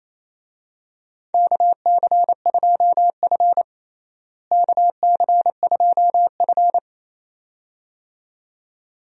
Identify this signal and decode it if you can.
Morse code, heard as a keyed tone, about 28 wpm, 700 Hz. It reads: KC2F KC2F
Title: USA Call Signs - 4 Characters - Repeated 2x - 28wpm